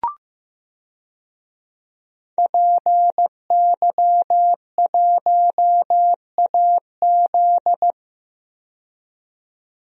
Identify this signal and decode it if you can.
Morse code, heard as a keyed tone, about 15 wpm, 700 Hz. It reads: PY1AZ
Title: DXCC-100 Call Signs - 5 Characters - 15wpm